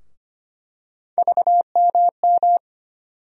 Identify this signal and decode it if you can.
Morse code, heard as a keyed tone, about 25 wpm, 700 Hz. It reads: VMM
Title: USA Call Sign Postfixes - 3 Letters 25wpm